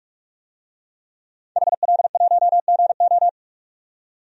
Morse code, heard as a keyed tone, about 42 wpm, 700 Hz. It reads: HL1GO
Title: DXCC-100 Call Signs - 5 Characters - 45wpm